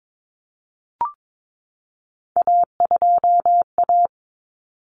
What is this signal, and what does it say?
Morse code, heard as a keyed tone, about 22 wpm, 700 Hz. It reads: A2A
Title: Common Cut Numbers - 3-Digit 22wpm